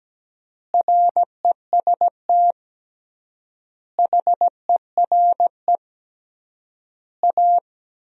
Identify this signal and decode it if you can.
Morse code, heard as a keyed tone, about 17 wpm, 700 Hz. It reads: REST HERE A
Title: 3x Word-Spacing - Sentences from Top 500 Words - 17wpm